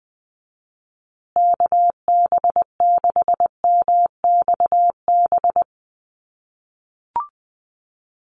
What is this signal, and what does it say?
Morse code, heard as a keyed tone, about 20 wpm, 700 Hz. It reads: KB6MXB
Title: USA Call Signs Repeated 2x 20wpm